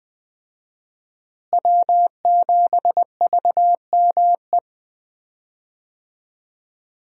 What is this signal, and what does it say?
Morse code, heard as a keyed tone, about 20 wpm, 700 Hz. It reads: W7VME
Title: USA Call Signs - 5 Characters - Repeated 3x - 20wpm